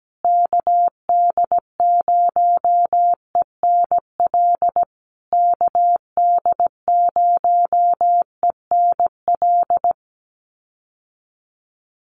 Morse code, heard as a keyed tone, about 17 wpm, 700 Hz. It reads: KD0ENL KD0ENL
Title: USA Call Signs Repeated 3x 17wpm